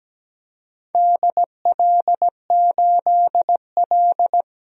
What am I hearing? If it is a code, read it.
Morse code, heard as a keyed tone, about 17 wpm, 700 Hz. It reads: DL8L
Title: DXCC-100 Call Signs - 4 Characters - 17wpm